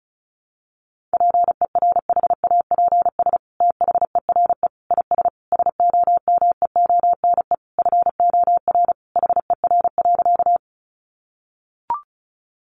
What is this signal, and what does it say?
Morse code, heard as a keyed tone, about 34 wpm, 700 Hz. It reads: PERHAPS THERE IS SOMEONE FOR HER.
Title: Sentences from Top 600 Words 35wpm